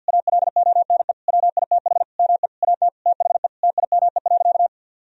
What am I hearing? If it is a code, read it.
Morse code, heard as a keyed tone, about 50 wpm, 700 Hz. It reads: ALONE WITH ME AT THE TIME.
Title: Sentences from Top 1000 Words - Review Part 4 - 50wpm